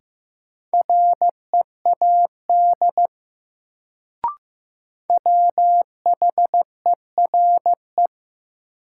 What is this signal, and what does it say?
Morse code, heard as a keyed tone, about 16 wpm, 700 Hz. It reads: READ WHERE
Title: Sentences from Top 200 Words - New Word Drill - 15wpm